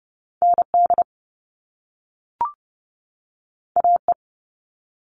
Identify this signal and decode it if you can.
Morse code, heard as a keyed tone, about 30 wpm, 700 Hz. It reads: ND AE
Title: Top Two Letter Combinations in Words 30wpm